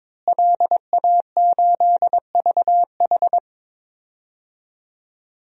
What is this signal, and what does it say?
Morse code, heard as a keyed tone, about 21 wpm, 700 Hz. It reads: LA8VH
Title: DXCC-100 Call Signs - 5 Characters - Repeated 2x - 22wpm